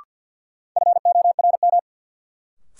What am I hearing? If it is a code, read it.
Morse code, heard as a keyed tone, about 49 wpm, 700 Hz. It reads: FORM
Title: Sets of 4 Words - New Word Drill - 50wpm